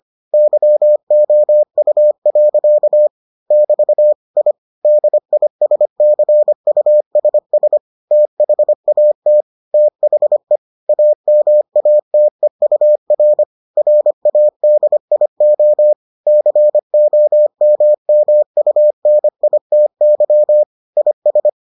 Morse code, heard as a keyed tone, about 25 wpm, 600 Hz. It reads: YOU. <BT> I DISCUSS THAT THE AMATEUR RADIO COMMUNITY IS